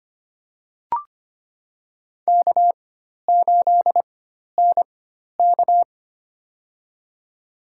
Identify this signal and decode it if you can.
Morse code, heard as a keyed tone, about 25 wpm, spaced as for 12 wpm, 700 Hz. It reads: K8NK
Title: USA Call Signs - 4 Characters - 4x Character Spacing - 25wpm